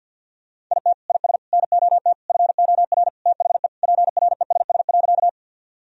Morse code, heard as a keyed tone, about 51 wpm, 700 Hz. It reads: IT IS NOT FOR THE PRESS.
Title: Sentences from Top 1000 Words - Review Part 1 - 50wpm